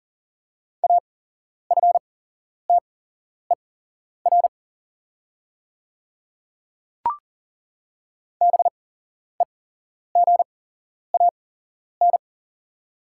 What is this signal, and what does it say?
Morse code, heard as a keyed tone, about 40 wpm, spaced as for 11 wpm, 690 Hz. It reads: AFTER BEGAN
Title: 8x Character Spacing - Top 500 Words - 40wpm